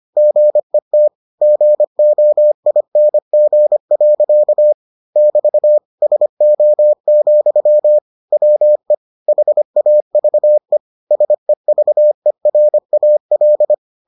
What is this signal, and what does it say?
Morse code, heard as a keyed tone, about 25 wpm, 600 Hz. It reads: GET GOING. <BT> SO, WE HAVE SEVERAL